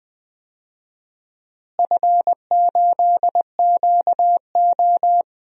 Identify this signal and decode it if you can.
Morse code, heard as a keyed tone, about 20 wpm, 700 Hz. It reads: F8QO
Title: DXCC-100 Call Signs - 4 Characters - Repeated 3x - 20wpm